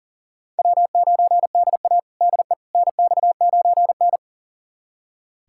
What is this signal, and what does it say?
Morse code, heard as a keyed tone, about 40 wpm, 700 Hz. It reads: W9DA DE NX9N
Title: USA Callsign DE Callsign - 4 Characters - 40wpm